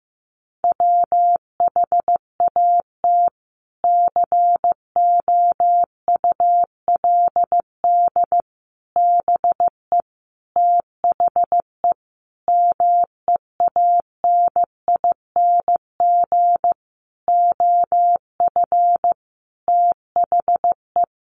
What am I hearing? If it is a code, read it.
Morse code, heard as a keyed tone, about 14 wpm, 700 Hz. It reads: WHAT COULD BE THE MEANING OF THE
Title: Sentences from Top 900 Words 15wpm